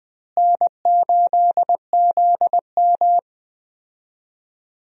Morse code, heard as a keyed tone, about 20 wpm, 700 Hz. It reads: N8ZM
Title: USA Call Signs - 4 Characters - Repeated 3x - 20wpm